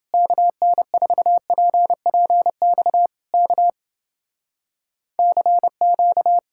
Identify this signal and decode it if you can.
Morse code, heard as a keyed tone, about 30 wpm, 700 Hz. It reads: KN4PPX K CQ